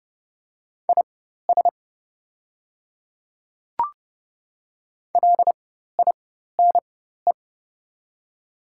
Morse code, heard as a keyed tone, about 30 wpm, spaced as for 14 wpm, 700 Hz. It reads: IS LINE
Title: 4x Character Spacing - Top 300 Words - 30wpm